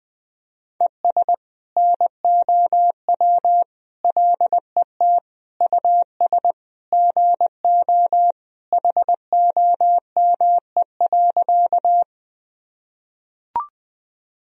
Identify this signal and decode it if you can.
Morse code, heard as a keyed tone, about 20 wpm, 700 Hz. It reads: ES NOW LET US GO HOME.
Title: Sentences from Top 300 Words and Abbreviations 20wpm